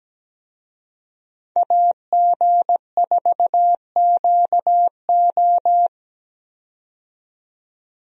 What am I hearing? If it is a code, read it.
Morse code, heard as a keyed tone, about 17 wpm, 700 Hz. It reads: AG4QO
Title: USA Call Signs - 5 Characters - 17wpm